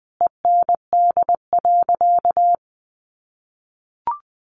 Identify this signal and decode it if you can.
Morse code, heard as a keyed tone, about 20 wpm, 700 Hz. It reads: END.
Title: Sentences from Top 1500 Words 20wpm